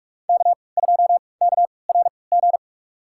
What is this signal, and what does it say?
Morse code, heard as a keyed tone, about 46 wpm, spaced as for 26 wpm, 700 Hz. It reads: K2KRG
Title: USA Call Signs - 5 Characters - 3x Character Spacing - 45wpm